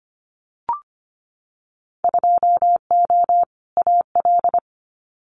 Morse code, heard as a keyed tone, about 25 wpm, 700 Hz. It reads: 2O AL
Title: Winter Field Day Exchanges 25wpm